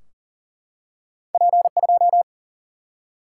Morse code, heard as a keyed tone, about 39 wpm, 700 Hz. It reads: P 2
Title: Single-Letter Single-Number 40wpm